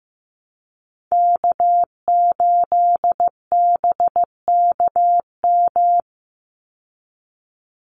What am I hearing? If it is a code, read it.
Morse code, heard as a keyed tone, about 15 wpm, 700 Hz. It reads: K8BKM
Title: Call Signs Repeated 2x - 15wpm